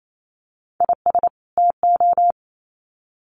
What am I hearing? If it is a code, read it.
Morse code, heard as a keyed tone, about 28 wpm, 700 Hz. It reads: IS TO
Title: Sets of 2 Words 28wpm